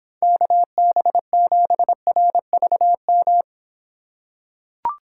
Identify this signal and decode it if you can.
Morse code, heard as a keyed tone, about 26 wpm, 700 Hz. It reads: KB7RVM